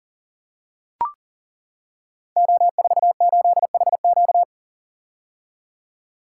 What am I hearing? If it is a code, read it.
Morse code, heard as a keyed tone, about 38 wpm, 700 Hz. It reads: O48HQ
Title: Code Groups - 5 Letter:Numbers - 40wpm